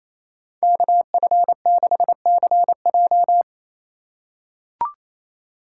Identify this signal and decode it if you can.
Morse code, heard as a keyed tone, about 28 wpm, 700 Hz. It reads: KF6CJ
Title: USA Call Signs - 5 Characters - 4x Character Spacing - 28wpm